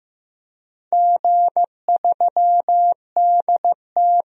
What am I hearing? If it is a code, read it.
Morse code, heard as a keyed tone, about 15 wpm, 700 Hz. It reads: G3DT
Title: DXCC-100 Call Signs - 4 Characters - 15wpm